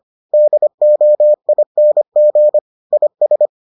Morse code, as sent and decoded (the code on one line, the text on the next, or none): -.. --- .. -. --. / .. ...
DOING IS